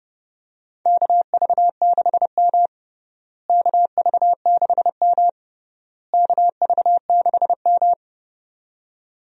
Morse code, heard as a keyed tone, about 32 wpm, 700 Hz. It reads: KV6M KV6M KV6M
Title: USA Call Signs - 4 Characters - Repeated 3x - 30wpm